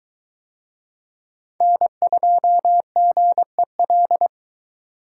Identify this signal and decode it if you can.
Morse code, heard as a keyed tone, about 23 wpm, 700 Hz. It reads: N2GEL